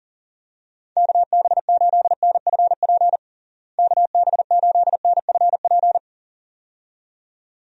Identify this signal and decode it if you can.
Morse code, heard as a keyed tone, about 40 wpm, 700 Hz. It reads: KB8NFP KB8NFP
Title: USA Call Signs - 6 Characters - Repeated 2x - 40wpm